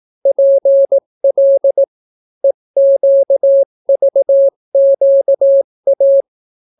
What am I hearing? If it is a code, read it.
Morse code, heard as a keyed tone, about 18 wpm, 550 Hz. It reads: PL EQVQA